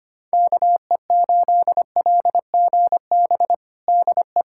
Morse code, heard as a keyed tone, about 25 wpm, 700 Hz. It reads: KE8LGB DE
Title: USA Callsign DE Callsign - 6 Characters - 25wpm